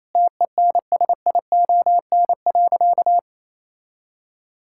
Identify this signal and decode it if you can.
Morse code, heard as a keyed tone, about 28 wpm, 700 Hz. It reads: TENSION.
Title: Sentences from Top 1600 Words 28wpm